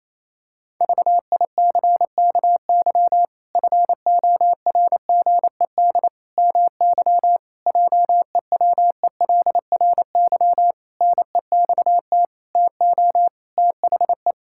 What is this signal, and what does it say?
Morse code, heard as a keyed tone, about 28 wpm, 700 Hz. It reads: VICKY FORGED MY JEWELRY NEXT TO THE
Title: Pangram Challenge v2 - 28wpm